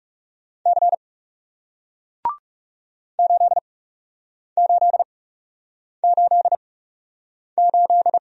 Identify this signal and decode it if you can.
Morse code, heard as a keyed tone, about 40 wpm, spaced as for 8 wpm, 700 Hz. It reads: C 8888